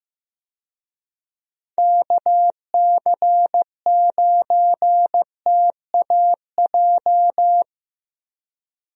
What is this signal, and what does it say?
Morse code, heard as a keyed tone, about 15 wpm, 700 Hz. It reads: KC9TAJ
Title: USA Call Signs - 6 Characters - 15wpm